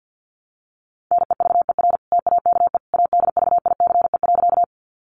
Morse code, heard as a keyed tone, about 50 wpm, 700 Hz. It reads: NEVER TAKE ADVICE.
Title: Sentences from Top 1100 Words 50wpm